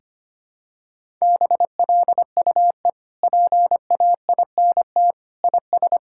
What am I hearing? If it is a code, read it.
Morse code, heard as a keyed tone, about 25 wpm, 700 Hz. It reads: BLUE PAINT IS
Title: Sentences from Top 1000 Words - Review Part 6 - 25wpm